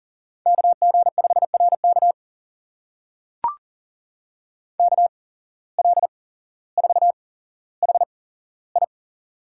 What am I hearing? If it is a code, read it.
Morse code, heard as a keyed tone, about 40 wpm, spaced as for 17 wpm, 700 Hz. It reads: KG5RK KL4HI
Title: USA Call Signs - 5 Characters - 8x Character Spacing - 40wpm